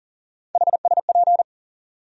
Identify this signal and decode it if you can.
Morse code, heard as a keyed tone, about 40 wpm, 700 Hz. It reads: HSP
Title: USA Call Sign Postfixes - 3 Letters 40wpm